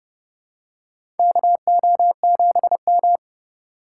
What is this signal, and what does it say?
Morse code, heard as a keyed tone, about 30 wpm, 700 Hz. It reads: KO7M
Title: USA Call Signs - 4 Characters - 3x Character Spacing - 30wpm